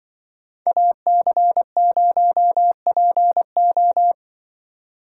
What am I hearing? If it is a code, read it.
Morse code, heard as a keyed tone, about 24 wpm, 700 Hz. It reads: AC0PO
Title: Speed Racing - USA 2x2 Call Signs 20wpm